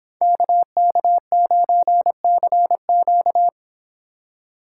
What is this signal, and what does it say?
Morse code, heard as a keyed tone, about 26 wpm, 700 Hz. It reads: KK9CQ